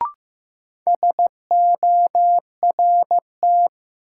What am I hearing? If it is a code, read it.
Morse code, heard as a keyed tone, about 15 wpm, 700 Hz. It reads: SORT
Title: Sets of 2 Words - New Word Drill - 15wpm